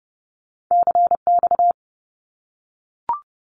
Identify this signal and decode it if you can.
Morse code, heard as a keyed tone, about 30 wpm, 700 Hz. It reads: CX
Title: Abbreviations 30wpm